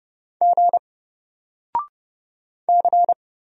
Morse code, heard as a keyed tone, about 30 wpm, 700 Hz. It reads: G C
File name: Single-Letter-Number-30wpm